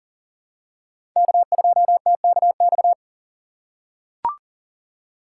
Morse code, heard as a keyed tone, about 38 wpm, 700 Hz. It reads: K2TKX
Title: USA Call Signs - 5 Characters - 2x Character Spacing - 40wpm